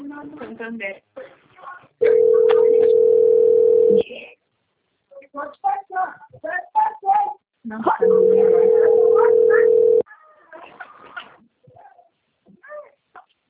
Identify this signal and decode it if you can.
Ringback tone (North America), 440 + 480 Hz, with speech.
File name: out- - - - - .33036